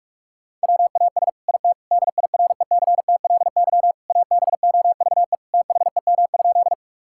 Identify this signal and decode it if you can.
Morse code, heard as a keyed tone, about 46 wpm, 690 Hz. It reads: WAS IT DIRECTLY ABOVE THEM?
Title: Sentences from Top 1000 Words - Review Part 5 - 45wpm